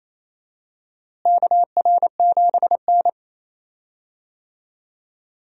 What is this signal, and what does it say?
Morse code, heard as a keyed tone, about 28 wpm, 700 Hz. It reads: KR7N
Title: USA Call Signs - 4 Characters - Repeated 2x - 28wpm